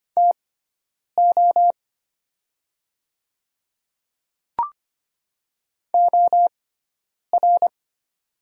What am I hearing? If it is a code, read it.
Morse code, heard as a keyed tone, about 25 wpm, spaced as for 9 wpm, 690 Hz. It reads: TO OR
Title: 6x Character Spacing - Top 200 Words - 25wpm